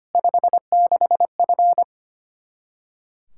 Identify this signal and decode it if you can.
Morse code, heard as a keyed tone, about 25 wpm, 700 Hz. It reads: 56F
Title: Practice QSO - Exchange 2 - Info - 25wpm